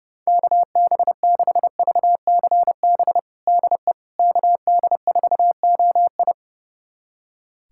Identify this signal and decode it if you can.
Morse code, heard as a keyed tone, about 30 wpm, 700 Hz. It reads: KB6VCB DE KD4OI
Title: USA Callsign DE Callsign 30wpm